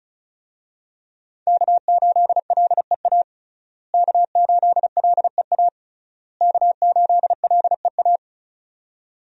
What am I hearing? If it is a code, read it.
Morse code, heard as a keyed tone, about 34 wpm, 700 Hz. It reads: K8LEA K8LEA K8LEA
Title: USA Call Signs - 5 Characters - Repeated 3x - 35wpm